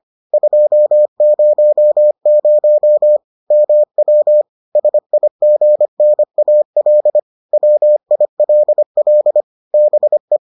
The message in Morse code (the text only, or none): 200 MW SIGNAL WILL BE